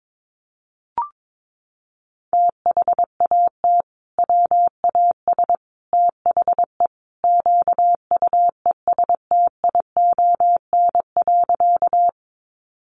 Morse code, heard as a keyed tone, about 22 wpm, 700 Hz. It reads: THAT WAS THE QUESTION.
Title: Sentences from Top 400 Words 22wpm